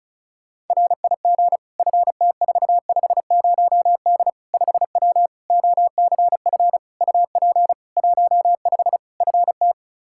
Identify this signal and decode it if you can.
Morse code, heard as a keyed tone, about 36 wpm, 700 Hz. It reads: RIG FT450D 5W OCF UP 15 FT
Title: K1USN SST QSO Practice 35wpm